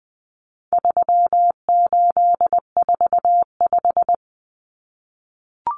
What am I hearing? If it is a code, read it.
Morse code, heard as a keyed tone, about 20 wpm, 700 Hz. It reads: 3845
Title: Common Cut Numbers - 4-Digit 20wpm